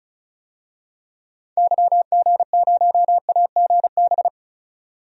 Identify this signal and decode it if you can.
Morse code, heard as a keyed tone, about 34 wpm, 700 Hz. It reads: YG0AGB
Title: DXCC-100 Call Signs - 6 Characters - 35wpm